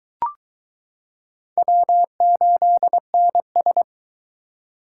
Morse code, heard as a keyed tone, about 21 wpm, 700 Hz. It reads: W8NS